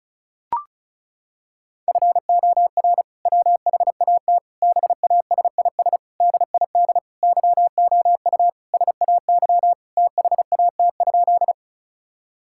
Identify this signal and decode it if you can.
Morse code, heard as a keyed tone, about 34 wpm, 700 Hz. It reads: FOR WHAT BASIS DID YOU SAY THAT?
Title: Sentences from Top 1000 Words - Review Part 5 - 35wpm